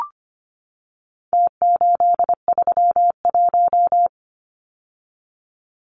Morse code, heard as a keyed tone, about 25 wpm, 700 Hz. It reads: T831
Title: Numbers - Contest Style - 4-Digit 25wpm